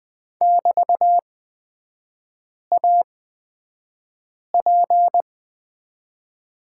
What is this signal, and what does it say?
Morse code, heard as a keyed tone, about 20 wpm, spaced as for 5 wpm, 700 Hz. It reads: <BT>AP
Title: 250 Lesson 40e - BK - Three Characters 20wpm